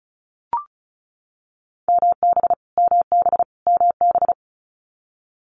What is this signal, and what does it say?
Morse code, heard as a keyed tone, about 36 wpm, 700 Hz. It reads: MB MB MB
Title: Multi-letter Phonemes Repeated 3x 35wpm